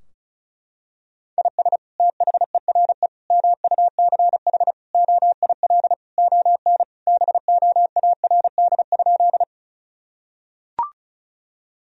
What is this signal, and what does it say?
Morse code, heard as a keyed tone, about 36 wpm, 700 Hz. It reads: IS THERE MUCH OIL ON BOARD?
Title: Sentences from Top 1000 Words - Review Part 4 - 35wpm